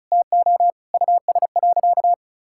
Morse code, heard as a keyed tone, about 36 wpm, 700 Hz. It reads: TO US.
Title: Sentences from Top 1600 Words 35wpm